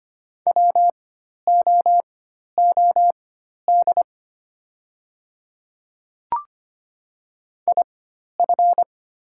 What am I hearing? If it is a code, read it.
Morse code, heard as a keyed tone, about 25 wpm, spaced as for 12 wpm, 700 Hz. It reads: WOOD IF
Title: 4x Character Spacing - Top 300 Words - 25wpm